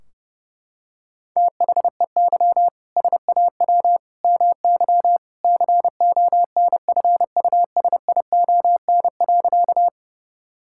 Morse code, heard as a keyed tone, about 32 wpm, 700 Hz. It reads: THEY SAW MY CONFUSION.
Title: Sentences from Top 1600 Words 30wpm